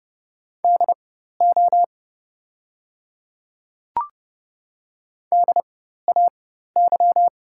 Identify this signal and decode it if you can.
Morse code, heard as a keyed tone, about 30 wpm, spaced as for 14 wpm, 700 Hz. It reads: DO DAY
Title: 4x Character Spacing - Top 200 Words - 30wpm